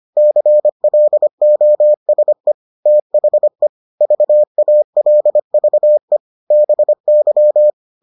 Morse code, heard as a keyed tone, about 25 wpm, 600 Hz. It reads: CLOSE THE VALVE BY